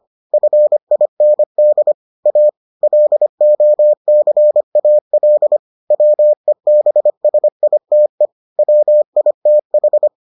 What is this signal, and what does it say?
Morse code, heard as a keyed tone, about 25 wpm, 600 Hz. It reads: FIND A LOCAL WEBSITE WITH